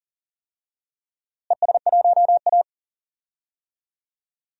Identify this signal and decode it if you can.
Morse code, heard as a keyed tone, about 40 wpm, 700 Hz. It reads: ES1A
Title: DXCC-100 Call Signs - 4 Characters - Repeated 3x - 40wpm